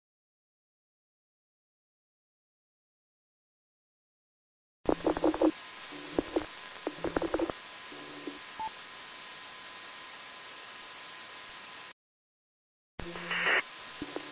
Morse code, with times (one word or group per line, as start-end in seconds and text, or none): none